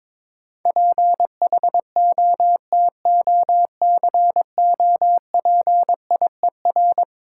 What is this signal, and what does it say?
Morse code, heard as a keyed tone, about 23 wpm, 690 Hz. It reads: PHOTOCOPIER
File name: Easy Words - Speed-Racing with Hints - 22wpm